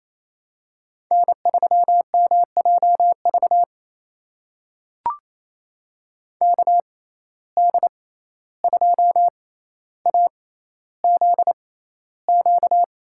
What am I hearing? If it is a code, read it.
Morse code, heard as a keyed tone, about 28 wpm, spaced as for 14 wpm, 700 Hz. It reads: N3MJV KD2AZQ
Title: USA Call Signs - 6x Character Spacing - 28wpm